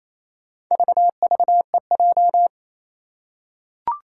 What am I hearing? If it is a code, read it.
Morse code, heard as a keyed tone, about 28 wpm, 700 Hz. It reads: VVEJ
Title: Four Challenging Letter-Number Combinations 28wpm